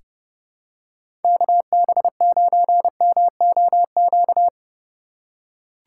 Morse code, heard as a keyed tone, about 29 wpm, 700 Hz. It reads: KB9MOQ
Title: USA Call Signs - 6 Characters - 6x Character Spacing - 30wpm